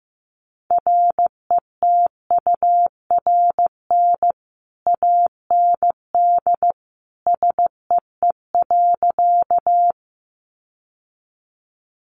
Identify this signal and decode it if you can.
Morse code, heard as a keyed tone, about 15 wpm, 700 Hz. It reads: RETURN AND SEE.
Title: Sentences from Top 800 Words 15wpm